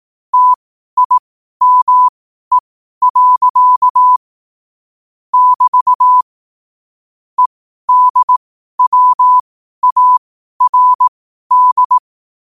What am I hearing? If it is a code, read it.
Morse code, heard as a keyed tone, about 18 wpm, spaced as for 12 wpm, 1000 Hz. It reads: TIME. <BT> EDWARD
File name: qotdfiles / 12wpm